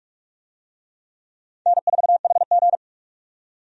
Morse code, heard as a keyed tone, about 46 wpm, 700 Hz. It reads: N4HG
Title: USA Call Signs - 4 Characters - 45wpm